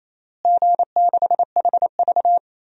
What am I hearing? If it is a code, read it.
Morse code, heard as a keyed tone, about 28 wpm, 700 Hz. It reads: G6HV
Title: DXCC-100 Call Signs - 4 Characters - Repeated 3x - 28wpm